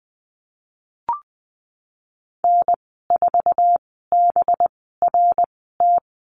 Morse code, heard as a keyed tone, about 19 wpm, spaced as for 14 wpm, 700 Hz. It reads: N4BRT
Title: 207 Lesson 33g - Letter j - Call Signs 20wpm